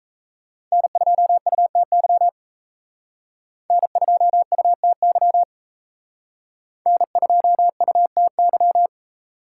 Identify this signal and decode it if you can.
Morse code, heard as a keyed tone, about 35 wpm, 700 Hz. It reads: N2UTY N2UTY N2UTY
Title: Speed Racing - USA 1x3 Call Signs 28wpm